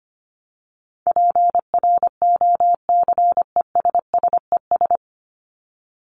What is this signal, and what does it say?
Morse code, heard as a keyed tone, about 25 wpm, 700 Hz. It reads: PROCESSES
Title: ES:S Suffix - Top 100 Words with Base Words 25wpm